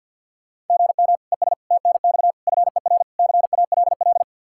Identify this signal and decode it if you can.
Morse code, heard as a keyed tone, about 49 wpm, 700 Hz. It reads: GM ES TNX FER CALL
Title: Practice QSO - Exchange 1 - Beginning - 50wpm